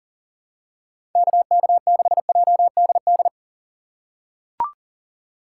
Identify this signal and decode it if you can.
Morse code, heard as a keyed tone, about 39 wpm, 700 Hz. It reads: KK6JDD
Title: USA Call Signs - 6 Characters - 6x Character Spacing - 40wpm